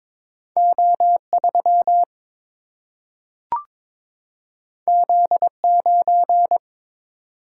Morse code, heard as a keyed tone, about 22 wpm, 700 Hz. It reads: O 3 Z 9
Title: Single-Letter Single-Number 22wpm